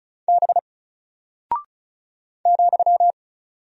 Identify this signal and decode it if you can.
Morse code, heard as a keyed tone, about 36 wpm, 700 Hz. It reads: B ,
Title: Single Character (Letter:Number:Punctuation:Prosign) Alt 35wpm